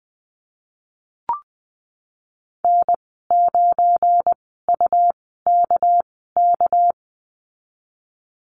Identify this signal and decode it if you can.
Morse code, heard as a keyed tone, about 20 wpm, spaced as for 14 wpm, 700 Hz. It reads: N9UKK
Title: 213 Lesson 34g - Number 8 - Call Signs 20wpm